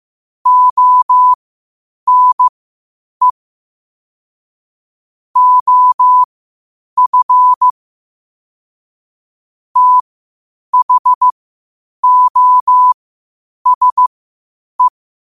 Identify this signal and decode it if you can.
Morse code, heard as a keyed tone, about 15 wpm, spaced as for 8 wpm, 1000 Hz. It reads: ONE OF THOSE
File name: qotdfiles / 8wpm